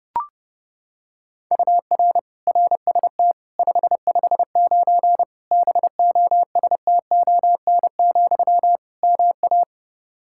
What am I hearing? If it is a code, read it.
Morse code, heard as a keyed tone, about 30 wpm, 700 Hz. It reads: UR RST 559 BOSTON, MA
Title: K1USN SST QSO Practice 30wpm